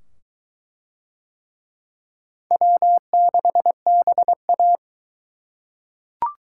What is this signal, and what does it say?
Morse code, heard as a keyed tone, about 23 wpm, 700 Hz. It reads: W6BA